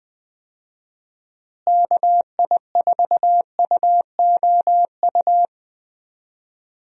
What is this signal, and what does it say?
Morse code, heard as a keyed tone, about 20 wpm, 700 Hz. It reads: KI4UOU
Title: USA Call Signs - 6 Characters - Repeated 3x - 20wpm